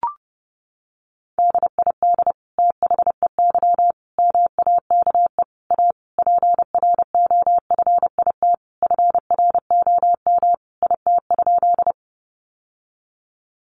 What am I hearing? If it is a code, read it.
Morse code, heard as a keyed tone, about 30 wpm, 700 Hz. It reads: DID THEY MAKE A PROFIT FROM IT?
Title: Sentences from Top 800 Words 30wpm